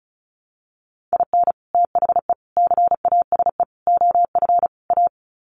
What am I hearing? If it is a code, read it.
Morse code, heard as a keyed tone, about 34 wpm, 700 Hz. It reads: IN THE CASE OF A
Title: Sets of 5 Words Encore 35wpm